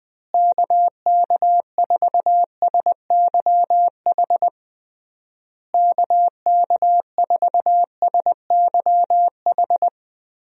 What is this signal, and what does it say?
Morse code, heard as a keyed tone, about 21 wpm, 700 Hz. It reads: KK4SYH KK4SYH
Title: USA Call Signs - 6 Characters - Repeated 3x - 20wpm